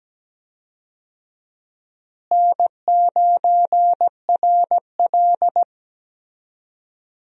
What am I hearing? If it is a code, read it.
Morse code, heard as a keyed tone, about 17 wpm, 700 Hz. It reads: N9RL
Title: USA Call Signs - 4 Characters - Repeated 2x - 17wpm